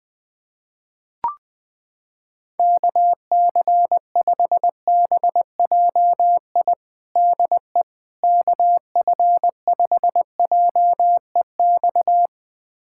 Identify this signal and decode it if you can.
Morse code, heard as a keyed tone, about 20 wpm, 700 Hz. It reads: KC5BJI DE KF5JEX
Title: USA Callsign DE Callsign - 6 Characters - 20wpm